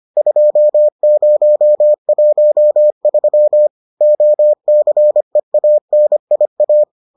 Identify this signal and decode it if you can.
Morse code, heard as a keyed tone, about 25 wpm, 600 Hz. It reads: 2013 OCEANIA